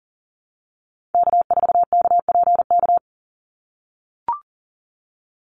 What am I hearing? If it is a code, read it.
Morse code, heard as a keyed tone, about 40 wpm, 700 Hz. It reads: K4KPK
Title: NA SOTA Activator Callsigns 40wpm